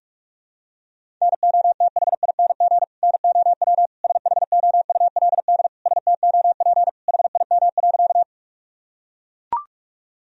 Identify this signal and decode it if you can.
Morse code, heard as a keyed tone, about 41 wpm, 700 Hz. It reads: NOTHING NOW SHOULD STOP HIM.
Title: Sentences from Top 1000 Words - Review Part 1 - 45wpm